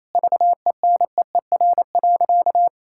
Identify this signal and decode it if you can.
Morse code, heard as a keyed tone, about 27 wpm, 700 Hz. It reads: VENEER.
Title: Pangram Challenge v2 - 28wpm